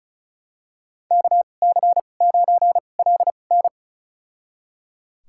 Morse code, heard as a keyed tone, about 34 wpm, spaced as for 25 wpm, 700 Hz. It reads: KC9LN
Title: USA Call Signs - 5 Characters - 2x Character Spacing - 35wpm